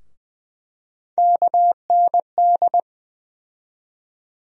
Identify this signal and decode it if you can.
Morse code, heard as a keyed tone, about 20 wpm, 700 Hz. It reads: KND
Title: USA Call Sign Postfixes - 3 Letters 20wpm